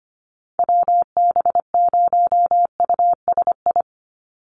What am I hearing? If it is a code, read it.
Morse code, heard as a keyed tone, about 25 wpm, 700 Hz. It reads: WB0USI
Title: NA SOTA Activator Callsigns 25wpm